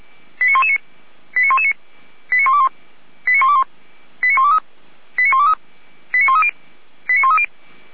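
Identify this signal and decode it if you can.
Five-tone selective call (ZVEI), heard as quick runs of five tones, tones of about 70 ms. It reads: 89119 89119 89121 89121 89123 89123 89130 89130
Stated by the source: Five-tone ZVEI selective call received by SDR, from the feuerwehr-orlamuende.org fire brigade audio archive.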